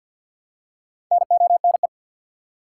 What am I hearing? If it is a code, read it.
Morse code, heard as a keyed tone, about 52 wpm, 700 Hz. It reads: NONE
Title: Four Letter Words - 50wpm